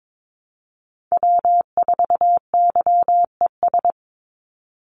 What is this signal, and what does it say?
Morse code, heard as a keyed tone, about 22 wpm, 700 Hz. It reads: W4YES
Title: Call Signs - 22wpm